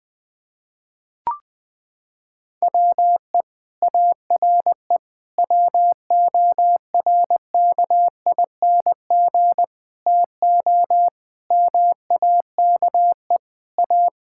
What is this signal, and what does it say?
Morse code, heard as a keyed tone, about 20 wpm, 700 Hz. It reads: WE ARE WORKING TO MAKE A
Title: Sentences from Top 1000 Words - Review Part 6 - 20wpm